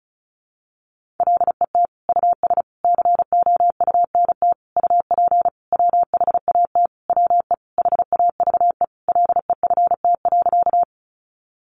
Morse code, heard as a keyed tone, about 34 wpm, 700 Hz. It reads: LET US COUNT UP WHAT WE HAVE LEFT.
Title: Sentences from Top 1500 Words 35wpm